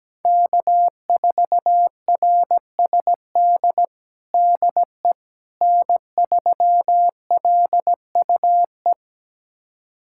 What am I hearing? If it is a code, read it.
Morse code, heard as a keyed tone, about 17 wpm, 700 Hz. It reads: K4RSD DE N3LUE
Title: USA Callsign DE Callsign - 5 Characters - 17wpm